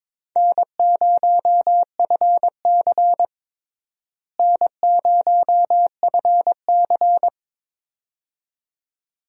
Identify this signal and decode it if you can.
Morse code, heard as a keyed tone, about 22 wpm, 700 Hz. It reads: N0FC N0FC
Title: USA Call Signs - 4 Characters - Repeated 2x - 22wpm